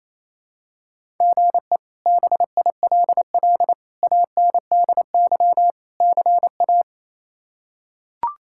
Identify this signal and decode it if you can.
Morse code, heard as a keyed tone, about 28 wpm, 700 Hz. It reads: GE BILL ANDY CA
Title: K1USN SST QSO Practice - 28wpm